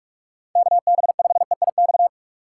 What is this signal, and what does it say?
Morse code, heard as a keyed tone, about 41 wpm, 700 Hz. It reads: KB5EIX